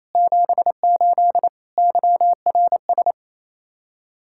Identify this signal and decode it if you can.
Morse code, heard as a keyed tone, about 35 wpm, 700 Hz. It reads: 78 YRS